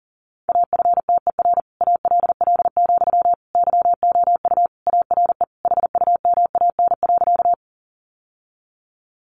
Morse code, heard as a keyed tone, about 40 wpm, 700 Hz. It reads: AFTER ALL, YOU ARE HUMAN.
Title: Sentences from Top 600 Words 40wpm